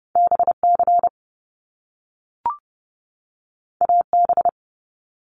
Morse code, heard as a keyed tone, about 30 wpm, 700 Hz. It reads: BC AB
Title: Canadian Provinces and Territories 30wpm